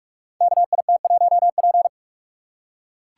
Morse code, heard as a keyed tone, about 44 wpm, 690 Hz. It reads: KIT1P
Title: Code Groups - 5 Letter:Numbers - 45wpm